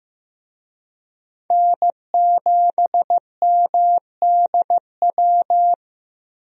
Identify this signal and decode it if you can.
Morse code, heard as a keyed tone, about 15 wpm, 700 Hz. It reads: N7MDW
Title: USA Call Signs - 5 Characters - Repeated 2x - 15wpm